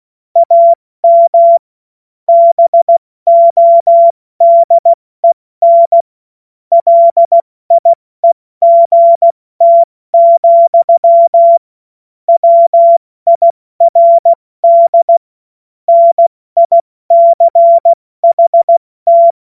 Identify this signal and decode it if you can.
Morse code, heard as a keyed tone, about 16 wpm, 680 Hz. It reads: AM BODEN LIEGT, WIRD NICHT
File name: Morsekurs / Witze / Witz 17. August 2021, 70 F80